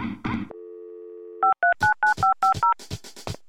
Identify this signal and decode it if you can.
DTMF, heard as telephone touch tones, tones of about 130 ms. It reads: 5399550